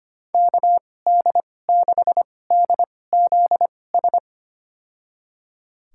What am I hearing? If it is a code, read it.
Morse code, heard as a keyed tone, about 25 wpm, spaced as for 18 wpm, 700 Hz. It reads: KD6DZS
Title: USA Call Signs - 6 Characters - 2x Character Spacing - 25wpm